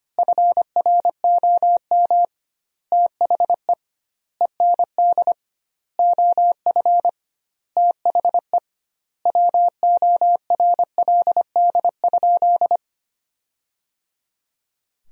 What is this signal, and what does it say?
Morse code, heard as a keyed tone, about 25 wpm, 700 Hz. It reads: FROM THE END OF THE WORLD?
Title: 2x Word-Spacing - Sentences from Top 200 Words - 25wpm